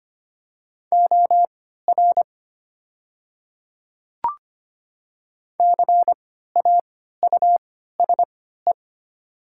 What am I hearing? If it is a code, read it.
Morse code, heard as a keyed tone, about 25 wpm, spaced as for 14 wpm, 700 Hz. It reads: OR CAUSE
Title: 3x Character Spacing - Top 300 Words - 25wpm